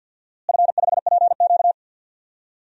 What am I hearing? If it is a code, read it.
Morse code, heard as a keyed tone, about 50 wpm, 700 Hz. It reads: F5PQ
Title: DXCC-100 Call Signs - 4 Characters - Repeated 2x - 50wpm